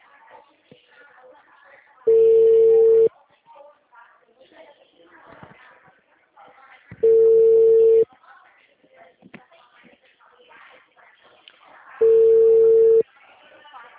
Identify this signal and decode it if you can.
Ringback tone (China, 450 Hz), with speech.
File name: out- -2002- - - .796